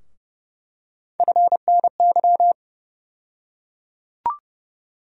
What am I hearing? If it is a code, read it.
Morse code, heard as a keyed tone, about 30 wpm, 700 Hz. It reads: FNY
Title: USA Call Sign Postfixes - 3 Letters 30wpm